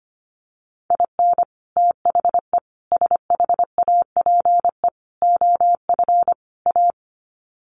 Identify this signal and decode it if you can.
Morse code, heard as a keyed tone, about 25 wpm, 700 Hz. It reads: IN THE SHAPE OF A
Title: Sets of 5 Words Encore 25wpm